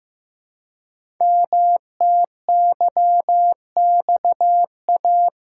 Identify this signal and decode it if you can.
Morse code, heard as a keyed tone, about 15 wpm, 700 Hz. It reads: MTYXA
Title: Code Groups - 5 Letters - 15wpm